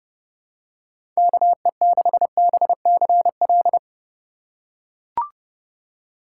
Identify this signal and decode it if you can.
Morse code, heard as a keyed tone, about 30 wpm, 700 Hz. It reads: KE6BCL
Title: USA Call Signs - 6 Characters - 3x Character Spacing - 30wpm